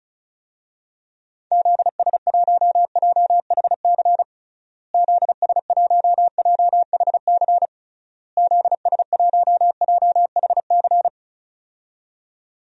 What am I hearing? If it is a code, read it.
Morse code, heard as a keyed tone, about 36 wpm, 700 Hz. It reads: ZS1JHC ZS1JHC ZS1JHC
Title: DXCC-100 Call Signs - 6 Characters - Repeated 3x - 35wpm